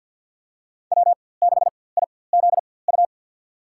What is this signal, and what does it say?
Morse code, heard as a keyed tone, about 51 wpm, spaced as for 23 wpm, 700 Hz. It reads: W6IZU
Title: USA Call Signs - 5 Characters - 4x Character Spacing - 50wpm